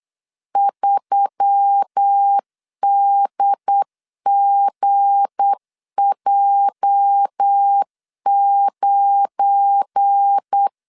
Morse code, heard as a keyed tone, about 8 wpm, 800 Hz. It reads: 3DGJ9
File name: LEZIONI / 8 YJ45 / 8 YJ 8wpm 3313